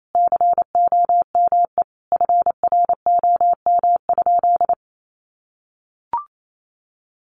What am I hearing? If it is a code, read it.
Morse code, heard as a keyed tone, about 28 wpm, 700 Hz. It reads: COME FROM?
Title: Sentences from Top 200 Words 28wpm